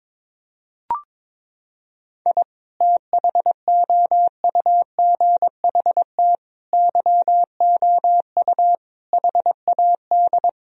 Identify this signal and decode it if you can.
Morse code, heard as a keyed tone, about 22 wpm, 700 Hz. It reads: I THOUGHT YOU HAD
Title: Sentences from Top 1000 Words - Review Part 4 - 22wpm